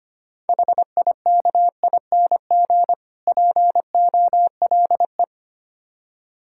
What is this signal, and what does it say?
Morse code, heard as a keyed tone, about 25 wpm, 700 Hz. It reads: HIKING POLE
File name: practice / 25wpm / Difficult Words - Speed-Racing with Hints - 25wpm